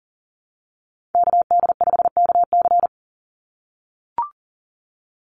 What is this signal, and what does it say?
Morse code, heard as a keyed tone, about 40 wpm, 700 Hz. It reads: KD5KC
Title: NA SOTA Activator Callsigns 40wpm